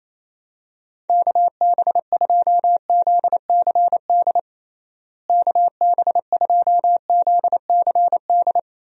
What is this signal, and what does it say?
Morse code, heard as a keyed tone, about 27 wpm, 700 Hz. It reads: KB2ZCD KB2ZCD
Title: USA Call Signs - 6 Characters - Repeated 2x - 28wpm